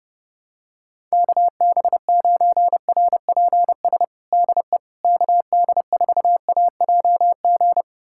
Morse code, heard as a keyed tone, about 32 wpm, 700 Hz. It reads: KB9RPS DE KD4AJG
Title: USA Callsign DE Callsign - 6 Characters - 30wpm